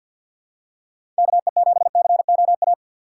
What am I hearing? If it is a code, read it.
Morse code, heard as a keyed tone, about 53 wpm, 700 Hz. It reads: KE7COA
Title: USA Call Signs - 6 Characters - 6x Character Spacing - 50wpm